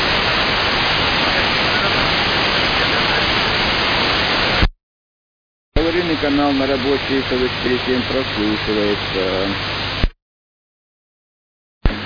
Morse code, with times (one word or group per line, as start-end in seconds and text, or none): none